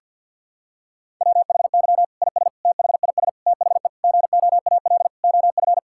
Morse code, heard as a keyed tone, about 48 wpm, 700 Hz. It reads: WHY IS THIS THE GOAL OF
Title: Sentences from Top 1000 Words - Review Part 5 - 50wpm